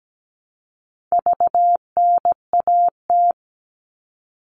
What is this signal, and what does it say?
Morse code, heard as a keyed tone, about 17 wpm, 700 Hz. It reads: VNAT
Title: All Cut Numbers - 4-Digit 17wpm